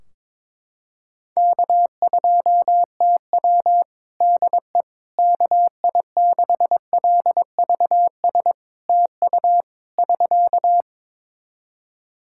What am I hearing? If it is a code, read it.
Morse code, heard as a keyed tone, about 22 wpm, 700 Hz. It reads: K2TW DE KI6LVS TU <SK>
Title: Practice QSO - Exchange 4 - Ending - 22wpm